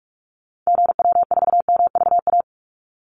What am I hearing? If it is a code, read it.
Morse code, heard as a keyed tone, about 44 wpm, 700 Hz. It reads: GW4MVA
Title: Call Signs - 45wpm